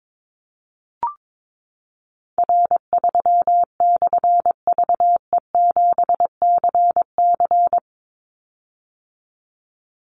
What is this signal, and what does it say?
Morse code, heard as a keyed tone, about 22 wpm, 700 Hz. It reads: R3/VE7CC
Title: Call Signs - 22wpm